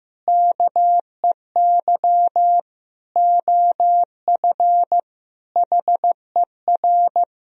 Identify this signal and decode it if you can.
Morse code, heard as a keyed tone, about 15 wpm, 700 Hz. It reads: KEY OF HER
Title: Sentences from Top 1000 Words - Review Part 4 - 15wpm